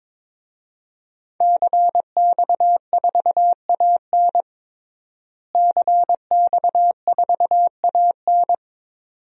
Morse code, heard as a keyed tone, about 22 wpm, 700 Hz. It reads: CX4AN CX4AN
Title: DXCC-100 Call Signs - 5 Characters - Repeated 2x - 22wpm